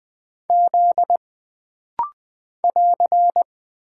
Morse code, heard as a keyed tone, about 20 wpm, 700 Hz. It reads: Z <AR>
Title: Single Character (Letter:Number:Punctuation:Prosign) - Rapid-Fire - 20wpm